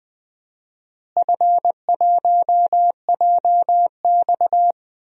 Morse code, heard as a keyed tone, about 20 wpm, 700 Hz. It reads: F1JX
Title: DXCC-100 Call Signs - 4 Characters - Repeated 3x - 20wpm